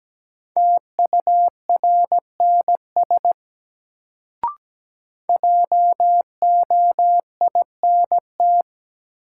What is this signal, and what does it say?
Morse code, heard as a keyed tone, about 17 wpm, 700 Hz. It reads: TURNS JOINT
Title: Sets of 5 Words Encore - New Word Drill - 17wpm